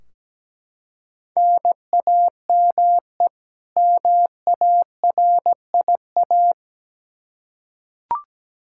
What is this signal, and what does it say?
Morse code, heard as a keyed tone, about 17 wpm, 700 Hz. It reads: NAME MARIA
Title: Practice QSO - Exchange 1 - Beginning - 17wpm